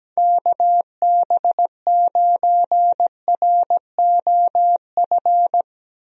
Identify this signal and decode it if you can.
Morse code, heard as a keyed tone, about 17 wpm, 700 Hz. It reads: KB9ROF
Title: USA Call Signs - 6 Characters - Repeated 3x - 17wpm